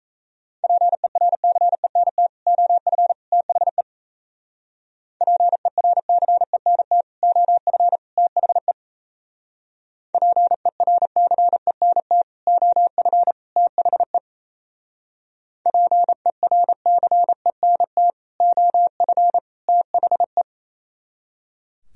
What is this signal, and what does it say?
Morse code, heard as a keyed tone, about 35 wpm, 700 Hz. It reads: PERCENT OF THE PERCENT OF THE PERCENT OF THE PERCENT OF THE